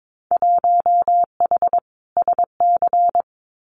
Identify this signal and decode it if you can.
Morse code, heard as a keyed tone, about 22 wpm, 700 Hz. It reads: 1H SC
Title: Winter Field Day Exchanges 22wpm